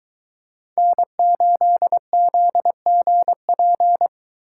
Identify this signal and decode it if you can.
Morse code, heard as a keyed tone, about 22 wpm, 700 Hz. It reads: N8ZGP